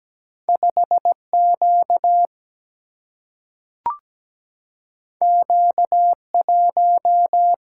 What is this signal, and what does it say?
Morse code, heard as a keyed tone, about 17 wpm, 700 Hz. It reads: 5Q Q1
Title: Single-Letter Single-Number 17wpm